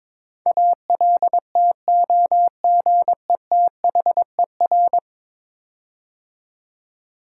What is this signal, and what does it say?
Morse code, heard as a keyed tone, about 23 wpm, 700 Hz. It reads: ALTOGETHER
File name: practice / 22wpm / 10 Letter Words - First 3 Letters Given - 22wpm